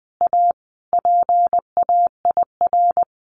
Morse code, heard as a keyed tone, about 20 wpm, 700 Hz. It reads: A PAIR
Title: Sentences from Top 1200 Words 20wpm